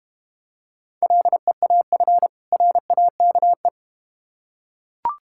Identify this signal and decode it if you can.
Morse code, heard as a keyed tone, about 31 wpm, 700 Hz. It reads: LEAF RAKE
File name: practice / 25wpm / Difficult Words - Speed-Racing with Hints - 25wpm